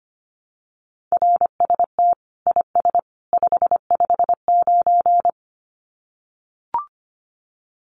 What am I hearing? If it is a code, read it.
Morse code, heard as a keyed tone, about 25 wpm, 700 Hz. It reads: RST IS 559
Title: Common QSO Phrases 25wpm